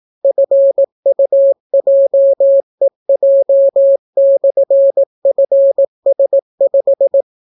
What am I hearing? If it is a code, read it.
Morse code, heard as a keyed tone, about 18 wpm, 550 Hz. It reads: FUJEJ/FS5